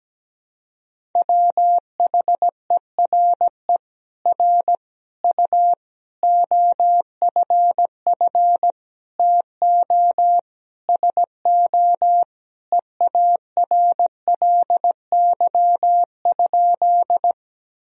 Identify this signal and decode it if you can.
Morse code, heard as a keyed tone, about 17 wpm, 700 Hz. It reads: WHERE R U OFF TO SO EARLY?
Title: Sentences from Top 400 Words and Abbreviations 17wpm